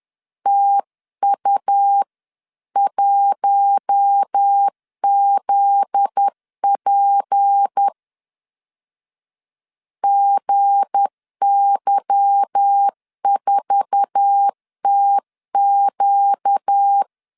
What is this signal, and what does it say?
Morse code, heard as a keyed tone, about 10 wpm, 800 Hz. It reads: TU1ZP GY4TQ
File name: LEZIONI / 8 YJ45 / 8 YJ4 10wpm 2633